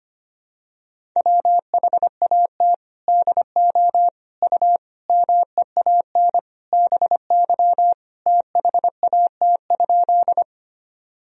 Morse code, heard as a keyed tone, about 25 wpm, 700 Hz. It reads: WHAT DO U MEAN BY THAT?
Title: Sentences from Top 200 Words and Abbreviations 25wpm